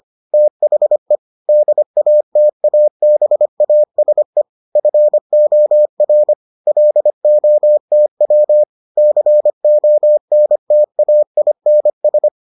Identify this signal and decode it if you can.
Morse code, heard as a keyed tone, about 25 wpm, 600 Hz. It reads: THE DATABASE FOR LOTW CONTAINS